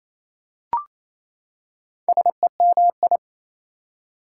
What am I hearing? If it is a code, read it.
Morse code, heard as a keyed tone, about 28 wpm, 700 Hz. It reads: SEMI
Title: Prefixes and Suffixes 28wpm